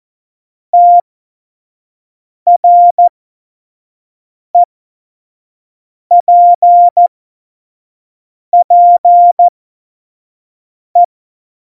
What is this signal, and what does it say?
Morse code, heard as a keyed tone, about 14 wpm, spaced as for 5 wpm, 700 Hz. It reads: TREPPE
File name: Morsekurs / Witze / Witz 24. November 2020, 25 F70